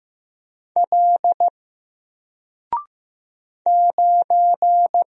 Morse code, heard as a keyed tone, about 15 wpm, 700 Hz. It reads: L 9
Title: Single Letter-Number Alt 15wpm